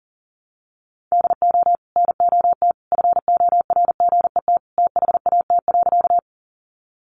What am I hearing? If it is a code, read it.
Morse code, heard as a keyed tone, about 40 wpm, 700 Hz. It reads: DO NOT FORGET THAT.
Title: Sentences from Top 1200 Words 40wpm